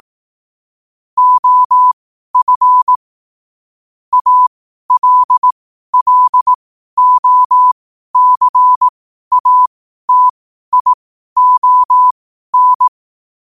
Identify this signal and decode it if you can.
Morse code, heard as a keyed tone, about 18 wpm, spaced as for 12 wpm, 1000 Hz. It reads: OF ALLOCATION
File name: qotdfiles / 12wpm